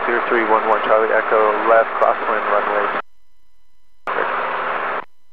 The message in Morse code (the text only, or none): none